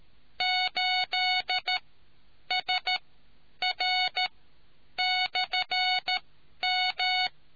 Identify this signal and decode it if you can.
Morse code, heard as a keyed tone, about 13 wpm, spaced as for 9 wpm, 740 Hz. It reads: 8SR/M